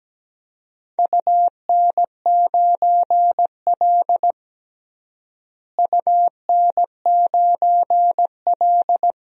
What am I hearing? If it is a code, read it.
Morse code, heard as a keyed tone, about 17 wpm, 700 Hz. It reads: UN9L UN9L
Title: DXCC-100 Call Signs - 4 Characters - Repeated 3x - 17wpm